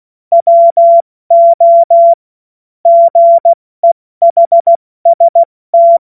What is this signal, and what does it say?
Morse code, heard as a keyed tone, about 16 wpm, 680 Hz. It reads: WO GEHST
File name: Witz-29-Juli-2025-70-F80